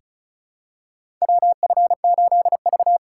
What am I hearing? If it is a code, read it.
Morse code, heard as a keyed tone, about 34 wpm, 700 Hz. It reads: WF8V
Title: USA Call Signs - 4 Characters - 4x Character Spacing - 35wpm